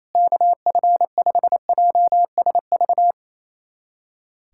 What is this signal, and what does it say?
Morse code, heard as a keyed tone, about 28 wpm, 700 Hz. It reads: KF5JSV
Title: USA Callsign DE Callsign - 6 Characters - 28wpm